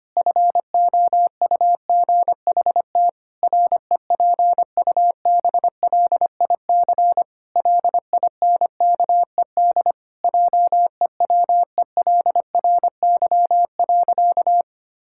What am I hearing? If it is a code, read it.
Morse code, heard as a keyed tone, about 25 wpm, 700 Hz. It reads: FOUGHT REPUBLIC LINKED JEWELRY.
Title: Pangram Challenge v2 - 25wpm